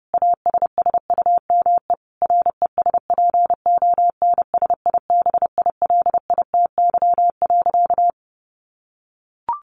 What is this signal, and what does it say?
Morse code, heard as a keyed tone, about 30 wpm, 700 Hz. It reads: ASSUME RESPONSIBILITY.
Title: Sentences from Top 1500 Words 30wpm